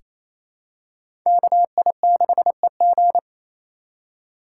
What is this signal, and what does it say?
Morse code, heard as a keyed tone, about 28 wpm, 700 Hz. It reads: KI6EG
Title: USA Call Signs - 4x Character Spacing - 28wpm